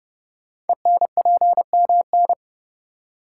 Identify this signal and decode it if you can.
Morse code, heard as a keyed tone, about 30 wpm, 700 Hz. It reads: ENPMN
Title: Code Groups - 5 Letter:Numbers - 30wpm